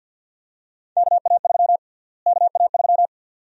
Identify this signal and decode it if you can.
Morse code, heard as a keyed tone, about 52 wpm, 700 Hz. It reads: KA3 KA3
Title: USA Call Sign Prefixes - Repeated 2x - 50wpm